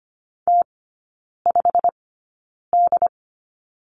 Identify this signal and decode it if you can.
Morse code, heard as a keyed tone, about 25 wpm, spaced as for 9 wpm, 700 Hz. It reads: T5D
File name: course / 25:10wpm / 093 Lesson 16e - Letter c - Three Characters 25wpm